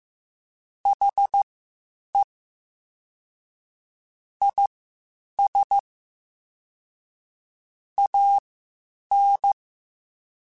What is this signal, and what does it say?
Morse code, heard as a keyed tone, about 15 wpm, spaced as for 8 wpm, 800 Hz. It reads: HE IS AN
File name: Quotes Spaced / Quotes Spaced-mp3s / Optimist sp @13